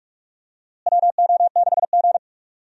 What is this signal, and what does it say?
Morse code, heard as a keyed tone, about 43 wpm, 700 Hz. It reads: WO6G
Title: USA Call Signs - 4 Characters - Repeated 3x - 45wpm